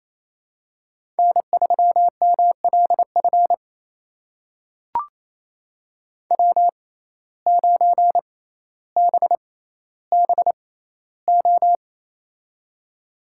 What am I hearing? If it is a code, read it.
Morse code, heard as a keyed tone, about 28 wpm, spaced as for 14 wpm, 700 Hz. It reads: N3MLF W9BBO
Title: USA Call Signs - 5 Characters - 6x Character Spacing - 28wpm